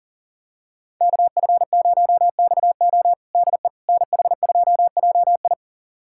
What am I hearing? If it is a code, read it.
Morse code, heard as a keyed tone, about 37 wpm, 700 Hz. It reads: KF0XO DE NH2JI
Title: USA Callsign DE Callsign - 5 Characters - 40wpm